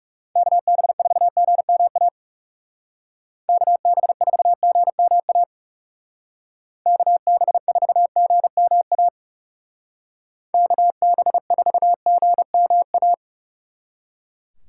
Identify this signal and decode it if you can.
Morse code, heard as a keyed tone, about 37 wpm, 700 Hz. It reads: KB4GMA KB4GMA KB4GMA KB4GMA
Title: Speed Racing - USA 2x3 Call Signs 30wpm